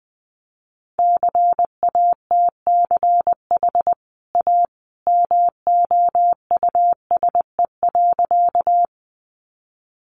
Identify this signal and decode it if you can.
Morse code, heard as a keyed tone, about 20 wpm, 700 Hz. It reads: CATCH A MOUSE.
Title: Sentences from Top 1500 Words 20wpm